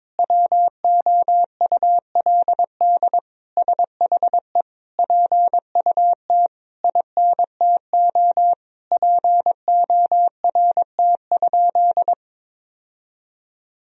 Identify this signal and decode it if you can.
Morse code, heard as a keyed tone, about 22 wpm, 700 Hz. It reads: WOULD SHE PUT INTO PORT?
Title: Sentences from Top 1000 Words - Review Part 1 - 22wpm